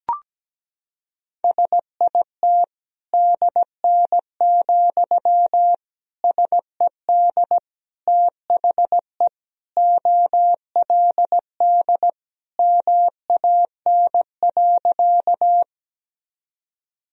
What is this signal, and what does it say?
Morse code, heard as a keyed tone, about 17 wpm, 700 Hz. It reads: SIT DN, SED THE OLD MAN.
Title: Sentences from Top 500 Words and Abbreviations 17wpm